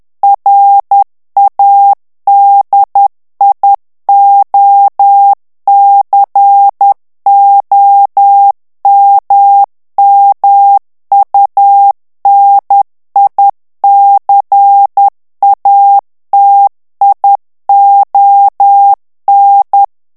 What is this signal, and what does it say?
Morse code, heard as a keyed tone, about 10 wpm, 800 Hz. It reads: RADIOCOMMUNICATION
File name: ANGLAIS texte clair 3 vitesse 10 mots minute